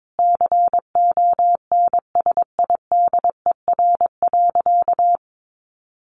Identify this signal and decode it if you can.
Morse code, heard as a keyed tone, about 22 wpm, 700 Hz. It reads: CONSIDER.
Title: Sentences from Top 900 Words 22wpm